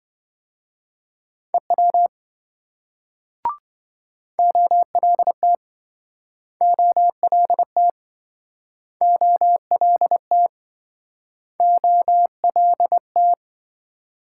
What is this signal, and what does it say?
Morse code, heard as a keyed tone, about 27 wpm, 700 Hz. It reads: EW OLT OLT OLT OLT